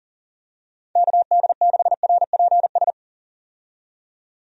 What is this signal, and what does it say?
Morse code, heard as a keyed tone, about 40 wpm, 700 Hz. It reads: KD6RPS
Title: USA Call Signs - 6 Characters - 40wpm